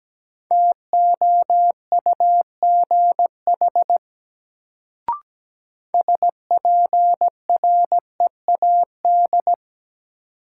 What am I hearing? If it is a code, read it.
Morse code, heard as a keyed tone, about 17 wpm, 700 Hz. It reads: TOUGH SPREAD
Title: Sentences from Top 1200 Words - New Word Drill - 17wpm